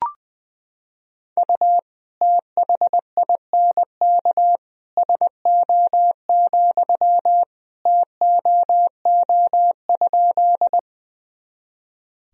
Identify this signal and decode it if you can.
Morse code, heard as a keyed tone, about 20 wpm, 700 Hz. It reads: U THINK SO, TOO?
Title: Sentences from Top 200 Words and Abbreviations 20wpm